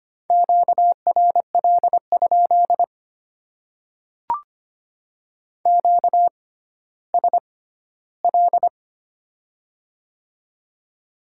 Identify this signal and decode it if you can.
Morse code, heard as a keyed tone, about 25 wpm, spaced as for 14 wpm, 700 Hz. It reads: QRL? QSL
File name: Top 90 QSO Elements - 6x Character Spacing - 25wpm